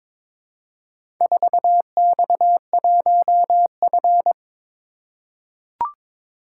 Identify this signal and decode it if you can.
Morse code, heard as a keyed tone, about 22 wpm, 700 Hz. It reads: 4X1F
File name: practice / 22wpm / Four Letter-Number Combinations - 22wpm